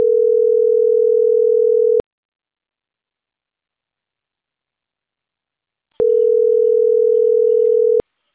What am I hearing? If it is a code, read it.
Ringback tone (North America), 440 + 480 Hz.